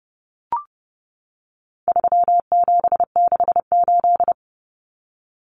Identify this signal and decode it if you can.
Morse code, heard as a keyed tone, about 30 wpm, 700 Hz. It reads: 3768
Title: Numbers - 4-Digit 30wpm